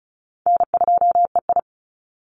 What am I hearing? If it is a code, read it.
Morse code, heard as a keyed tone, about 34 wpm, 700 Hz. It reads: N2EI
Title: Easy Call Signs Repeated 3x - 35wpm